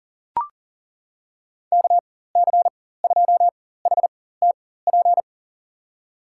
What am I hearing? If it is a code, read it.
Morse code, heard as a keyed tone, about 37 wpm, spaced as for 18 wpm, 700 Hz. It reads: KC2HTP
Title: USA Call Signs - 6 Characters - 4x Character Spacing - 40wpm